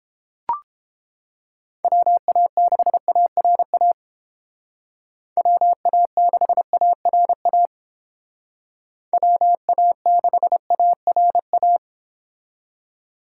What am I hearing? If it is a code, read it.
Morse code, heard as a keyed tone, about 32 wpm, 700 Hz. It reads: WA6ARA WA6ARA WA6ARA